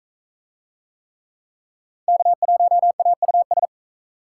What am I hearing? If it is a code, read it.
Morse code, heard as a keyed tone, about 39 wpm, 700 Hz. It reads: K1AUS